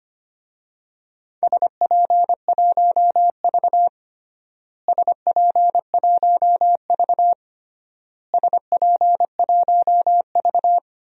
Morse code, heard as a keyed tone, about 25 wpm, 700 Hz. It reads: SP1V SP1V SP1V
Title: DXCC-100 Call Signs - 4 Characters - Repeated 3x - 25wpm